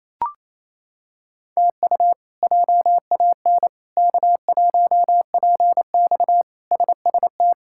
Morse code, heard as a keyed tone, about 28 wpm, 700 Hz. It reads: TU JAN K1PX SST
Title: K1USN SST QSO Practice - 28wpm